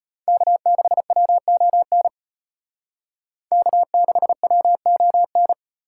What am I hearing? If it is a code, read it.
Morse code, heard as a keyed tone, about 32 wpm, 700 Hz. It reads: K6WON K6WON